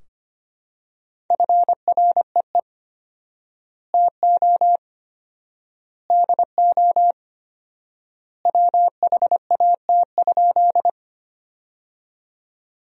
Morse code, heard as a keyed tone, about 25 wpm, 700 Hz. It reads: FREE TO DO WHAT?
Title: 4x Word-Spacing - Sentences from Top 500 Words - 25wpm